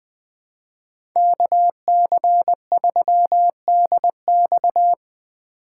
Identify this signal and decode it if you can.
Morse code, heard as a keyed tone, about 20 wpm, 700 Hz. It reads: KC3DX
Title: USA Call Signs - 4x Character Spacing - 20wpm